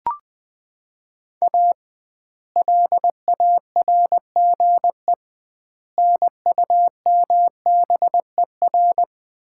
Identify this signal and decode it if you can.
Morse code, heard as a keyed tone, about 21 wpm, 700 Hz. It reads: A LARGE NUMBER
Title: 2x Word-Spacing - Sets of 4 Words - 20wpm